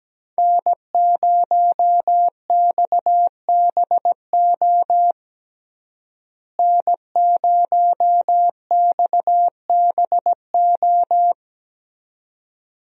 Morse code, heard as a keyed tone, about 17 wpm, 700 Hz. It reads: N0XBO N0XBO
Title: USA Call Signs - 5 Characters - Repeated 2x - 17wpm